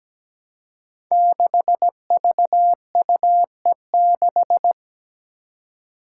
Morse code, heard as a keyed tone, about 17 wpm, 700 Hz. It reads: 6VUE6
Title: Five Challenging Letter-Number Combinations 17wpm